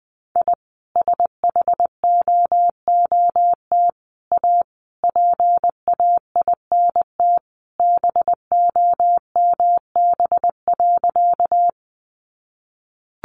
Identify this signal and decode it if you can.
Morse code, heard as a keyed tone, about 20 wpm, 700 Hz. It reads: I SHOOT A PAINT BOMB.
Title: A Sound Of Thunder 20wpm